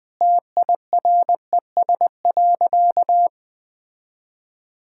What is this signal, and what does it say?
Morse code, heard as a keyed tone, about 20 wpm, 700 Hz. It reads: TIRES.
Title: Easy Sentences with Hints - 20wpm